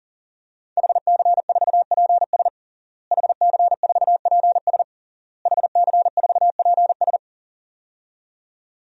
Morse code, heard as a keyed tone, about 40 wpm, 700 Hz. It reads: HC4PS HC4PS HC4PS
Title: DXCC-100 Call Signs - 5 Characters - Repeated 3x - 40wpm